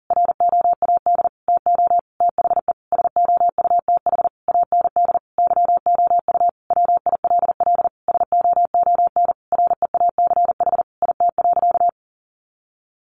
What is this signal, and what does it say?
Morse code, heard as a keyed tone, about 40 wpm, 700 Hz. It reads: ROAD TO THE SOUTH AND YOU WILL SOON REACH IT.
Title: Sentences from Top 400 Words 40wpm